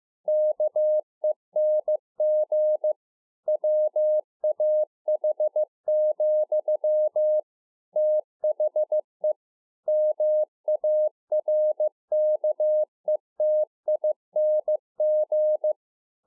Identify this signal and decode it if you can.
Morse code, heard as a keyed tone, about 15 wpm, 610 Hz. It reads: KENG WAH, THE MARKETING